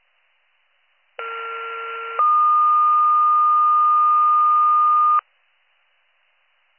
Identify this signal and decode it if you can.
Two-tone page, heard as a one-second tone, then a three-second tone, about 1370 then 1220 Hz. It is heard from 1.2 to 5.2 s.